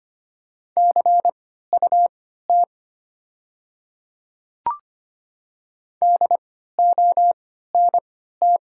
Morse code, heard as a keyed tone, about 25 wpm, spaced as for 14 wpm, 700 Hz. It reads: CUT DONT
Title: 3x Character Spacing - Top 300 Words - 25wpm